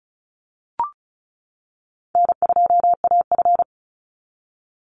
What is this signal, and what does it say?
Morse code, heard as a keyed tone, about 34 wpm, 700 Hz. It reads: N2AF
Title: NA SOTA Chaser Callsigns 35wpm